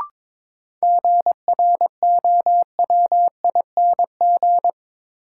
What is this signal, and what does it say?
Morse code, heard as a keyed tone, about 22 wpm, 700 Hz. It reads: GROWING
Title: Sets of 5 Words Encore - New Word Drill - 22wpm